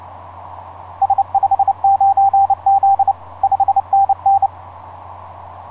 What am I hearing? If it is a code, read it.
Morse code, heard as a keyed tone, about 28 wpm, 790 Hz, with radio noise. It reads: S59Z 5NN
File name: s79gm s59z etc 10m cw 30mar2010 1126